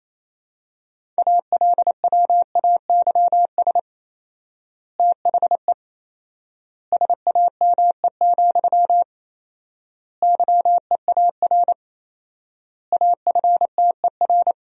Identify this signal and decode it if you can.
Morse code, heard as a keyed tone, about 28 wpm, 700 Hz. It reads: ALWAYS THE SAME, YEAR AFTER
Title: 4x Word-Spacing - Sentences from Top 300 Words - 28wpm